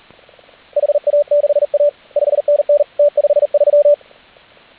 Morse code, heard as a keyed tone, about 40 wpm, 580 Hz, with radio noise. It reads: HA6A 5NN T53